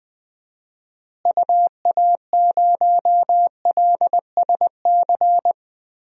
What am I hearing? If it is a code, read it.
Morse code, heard as a keyed tone, about 20 wpm, 700 Hz. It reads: UA0LSC
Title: DXCC-100 Call Signs - 6 Characters - Repeated 3x - 20wpm